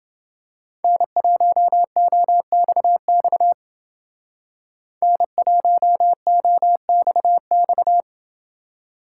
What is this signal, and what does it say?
Morse code, heard as a keyed tone, about 29 wpm, 700 Hz. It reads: N1OXX N1OXX